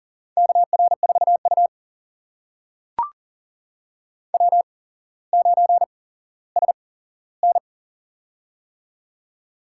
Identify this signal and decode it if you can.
Morse code, heard as a keyed tone, about 39 wpm, spaced as for 17 wpm, 700 Hz. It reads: KR4U W9SN
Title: USA Call Signs - 4 Characters - 8x Character Spacing - 40wpm